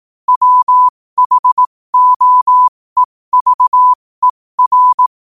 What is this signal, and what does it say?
Morse code, heard as a keyed tone, about 18 wpm, spaced as for 15 wpm, 1000 Hz. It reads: WHOEVER
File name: qotdfiles / 15wpm